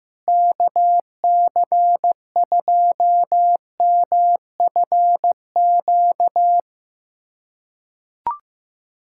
Morse code, heard as a keyed tone, about 15 wpm, 700 Hz. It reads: KC2MFQ
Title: USA Callsign DE Callsign 15wpm